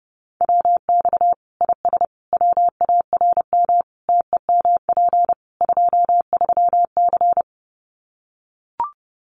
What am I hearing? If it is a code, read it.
Morse code, heard as a keyed tone, about 30 wpm, 700 Hz. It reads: WX IS WARM TEMP 23C
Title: Common QSO Phrases 30wpm